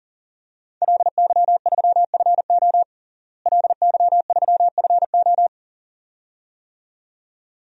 Morse code, heard as a keyed tone, about 40 wpm, 690 Hz. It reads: LY3FO LY3FO
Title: DXCC-100 Call Signs - 5 Characters - Repeated 2x - 40wpm